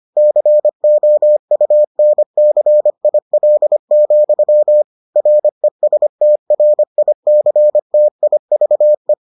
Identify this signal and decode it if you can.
Morse code, heard as a keyed tone, about 25 wpm, 600 Hz. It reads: COUNCIL, RESTRICTIVE